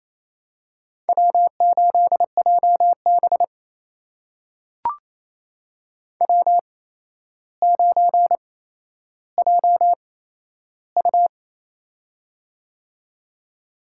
Morse code, heard as a keyed tone, about 28 wpm, spaced as for 12 wpm, 700 Hz. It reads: W8JB W9JU
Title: USA Call Signs - 4 Characters - 8x Character Spacing - 28wpm